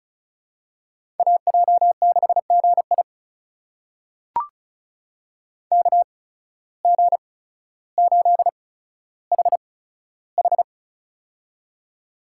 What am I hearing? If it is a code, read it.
Morse code, heard as a keyed tone, about 35 wpm, spaced as for 15 wpm, 690 Hz. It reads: AJ6GI KG8HH
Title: USA Call Signs - 5 Characters - 8x Character Spacing - 35wpm